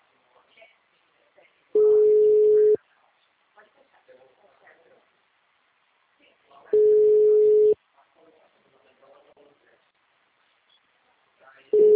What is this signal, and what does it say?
Ringback tone (Europe, 425 Hz), with speech.